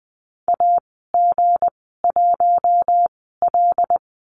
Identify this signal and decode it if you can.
Morse code, heard as a keyed tone, about 20 wpm, spaced as for 15 wpm, 700 Hz. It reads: AG1L
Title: 231 Lesson 37g - Letter x - Call Signs 20wpm